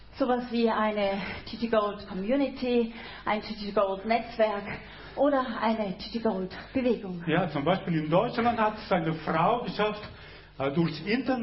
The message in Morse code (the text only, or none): none